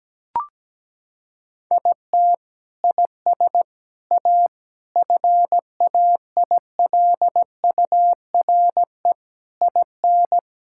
Morse code, heard as a keyed tone, about 17 wpm, 700 Hz. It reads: IT IS A FAILURE IN
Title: Sentences from Top 1000 Words - Review Part 6 - 17wpm